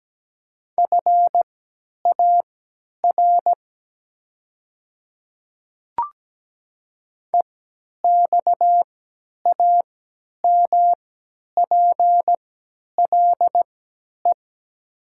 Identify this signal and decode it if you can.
Morse code, heard as a keyed tone, about 17 wpm, spaced as for 10 wpm, 700 Hz. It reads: FAR EXAMPLE
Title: 3x Character Spacing - Top 500 Words - 17wpm